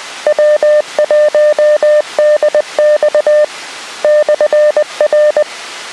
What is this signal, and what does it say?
Morse code, heard as a keyed tone, about 20 wpm, 600 Hz. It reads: W1DX /R